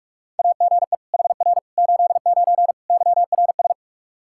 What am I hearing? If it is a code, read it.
Morse code, heard as a keyed tone, about 46 wpm, 700 Hz. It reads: AGE HR 89 YRS
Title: Practice QSO - Exchange 3 - More Info - 45wpm